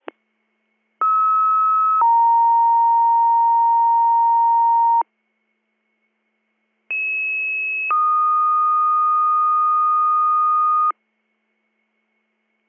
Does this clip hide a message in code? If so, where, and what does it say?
Yes: two-tone page from 1.0 to 10.9 s, heard as a one-second tone, then a three-second tone, about 1290 then 930 Hz, about 2600 then 1230 Hz.